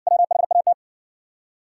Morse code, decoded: WHAT

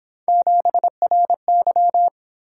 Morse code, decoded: 7RY

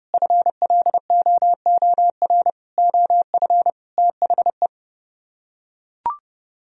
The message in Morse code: FLOOR OF THE

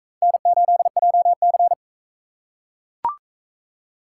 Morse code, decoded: N8JC

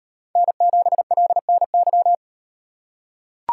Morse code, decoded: N7LNY

